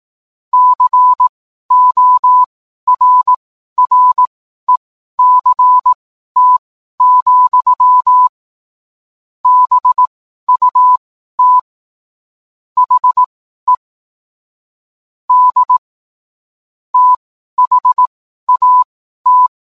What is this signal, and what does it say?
Morse code, heard as a keyed tone, about 18 wpm, spaced as for 12 wpm, 1000 Hz. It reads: CORRECT, BUT HE D THAT